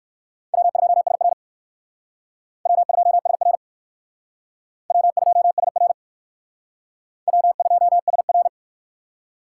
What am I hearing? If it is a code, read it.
Morse code, heard as a keyed tone, about 56 wpm, 700 Hz. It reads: W2SR W2SR W2SR W2SR
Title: Speed Racing - USA 1x2 Call Signs 45wpm